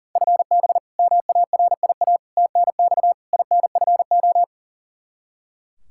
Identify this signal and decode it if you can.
Morse code, heard as a keyed tone, about 40 wpm, 690 Hz. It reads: FB MARIA TNX INFO
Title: Practice QSO - Exchange 3 - More Info - 40wpm